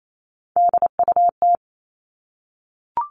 Morse code, heard as a keyed tone, about 28 wpm, 700 Hz. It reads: DUT
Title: All Cut Numbers - 3-Digit 28wpm